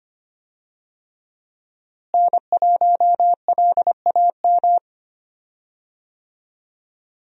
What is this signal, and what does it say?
Morse code, heard as a keyed tone, about 25 wpm, 700 Hz. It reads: N1LAM
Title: USA Call Signs - 5 Characters - Repeated 3x - 25wpm